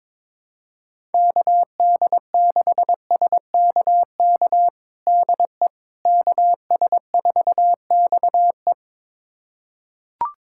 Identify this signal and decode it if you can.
Morse code, heard as a keyed tone, about 22 wpm, 700 Hz. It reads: KD6SKK DE KS4XE
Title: USA Callsign DE Callsign 22wpm